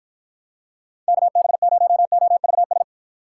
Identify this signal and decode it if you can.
Morse code, heard as a keyed tone, about 52 wpm, 700 Hz. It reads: KB0OVS